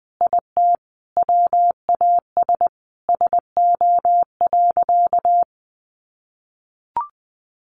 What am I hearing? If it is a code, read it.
Morse code, heard as a keyed tone, about 20 wpm, 700 Hz. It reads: IT WAS SO.
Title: Sentences from Top 1100 Words 20wpm